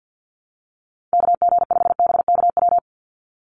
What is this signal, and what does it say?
Morse code, heard as a keyed tone, about 50 wpm, 700 Hz. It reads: KG5BKW
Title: USA Call Signs 50wpm